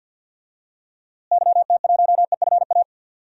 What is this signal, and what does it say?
Morse code, heard as a keyed tone, about 47 wpm, 700 Hz. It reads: YT1EFA
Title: DXCC-100 Call Signs - 6 Characters - Repeated 2x - 50wpm